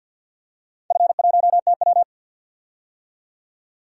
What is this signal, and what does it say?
Morse code, heard as a keyed tone, about 51 wpm, 700 Hz. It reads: F1TW
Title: DXCC-100 Call Signs - 4 Characters - 50wpm